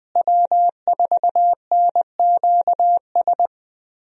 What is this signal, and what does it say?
Morse code, heard as a keyed tone, about 20 wpm, 700 Hz. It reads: W4NQS
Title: USA Call Signs - 5 Characters - Repeated 2x - 20wpm